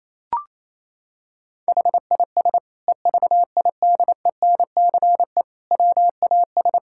Morse code, heard as a keyed tone, about 28 wpm, 700 Hz. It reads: HIS EVIDENCE WAS
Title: Sentences from Top 1600 Words 28wpm